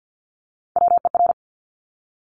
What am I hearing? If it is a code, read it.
Morse code, heard as a keyed tone, about 50 wpm, 700 Hz. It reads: WER
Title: Top Three Letter Combinations in Words 50wpm